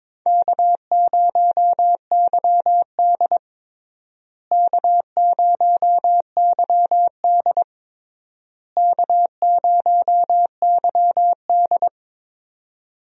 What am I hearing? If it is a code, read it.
Morse code, heard as a keyed tone, about 23 wpm, 700 Hz. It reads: K0YD K0YD K0YD
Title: USA Call Signs - 4 Characters - Repeated 3x - 22wpm